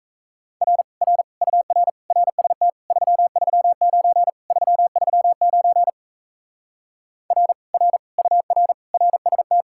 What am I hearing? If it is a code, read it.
Morse code, heard as a keyed tone, about 38 wpm, 700 Hz. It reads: R R UR RST 339 339 R R UR RST